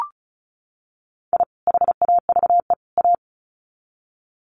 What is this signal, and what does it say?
Morse code, heard as a keyed tone, about 34 wpm, 700 Hz. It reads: I HAVE A
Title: Sets of 3 Words 35wpm